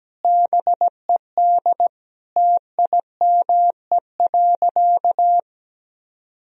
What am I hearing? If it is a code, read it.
Morse code, heard as a keyed tone, about 17 wpm, 700 Hz. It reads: BED TIME.
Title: Sentences from Top 1000 Words - Review Part 1 - 17wpm